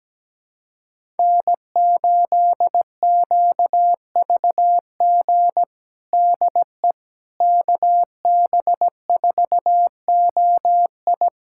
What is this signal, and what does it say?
Morse code, heard as a keyed tone, about 17 wpm, 700 Hz. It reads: N8QVG DE KB4OI
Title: USA Callsign DE Callsign - 5 Characters - 17wpm